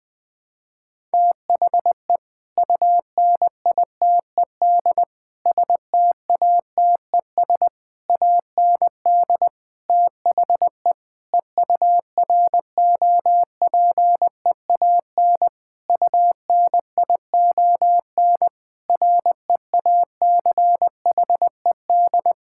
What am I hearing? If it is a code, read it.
Morse code, heard as a keyed tone, about 19 wpm, 700 Hz. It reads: THE UNITED STATES AND THE EUROPEAN UNION REACHED
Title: Sentences with 5-Word Phrases - 20wpm